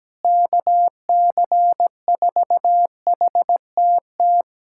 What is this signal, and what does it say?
Morse code, heard as a keyed tone, about 17 wpm, 700 Hz. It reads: KC4HTT
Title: USA Callsign DE Callsign 17wpm